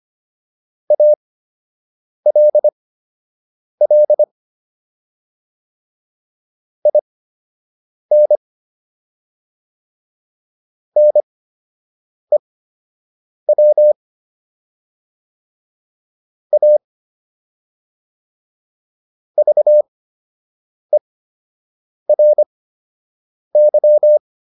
Morse code, heard as a keyed tone, about 25 wpm, spaced as for 7 wpm, 600 Hz. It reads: ALL IN NEW A VERY